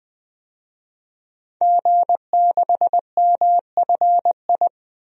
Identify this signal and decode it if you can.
Morse code, heard as a keyed tone, about 20 wpm, 700 Hz. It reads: G6MFI
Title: DXCC-100 Call Signs - 5 Characters - 20wpm